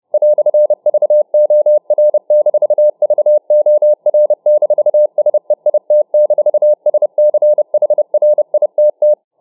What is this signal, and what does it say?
Morse code, heard as a keyed tone, about 29 wpm, 600 Hz. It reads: "VOR-VOR-SEIT-SCHRITT